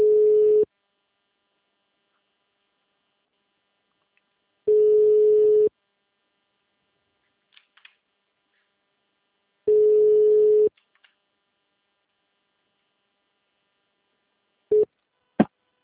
Ringback tone (Europe, 425 Hz).
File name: out- -2011- - - .431